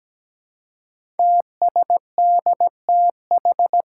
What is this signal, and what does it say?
Morse code, heard as a keyed tone, about 17 wpm, 700 Hz. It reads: TSDTH
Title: Five Challenging Letter-Number Combinations 17wpm